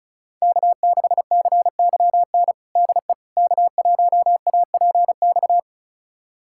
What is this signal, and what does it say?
Morse code, heard as a keyed tone, about 36 wpm, 700 Hz. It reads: K6CYN DE K1APX
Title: USA Callsign DE Callsign - 5 Characters - 35wpm